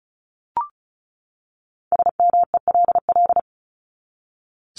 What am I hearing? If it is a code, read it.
Morse code, heard as a keyed tone, about 34 wpm, 700 Hz. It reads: SMELL
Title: Top 1501-1750 Words 35wpm